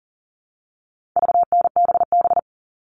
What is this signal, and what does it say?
Morse code, heard as a keyed tone, about 40 wpm, 700 Hz. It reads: VNBB
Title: All Cut Numbers - 4-Digit 40wpm